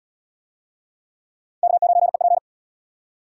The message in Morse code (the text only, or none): K9EP